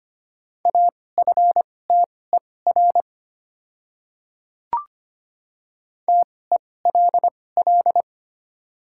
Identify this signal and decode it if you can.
Morse code, heard as a keyed tone, about 25 wpm, spaced as for 18 wpm, 700 Hz. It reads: AFTER TELL
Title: 2x Character Spacing - Top 200 Words - 25wpm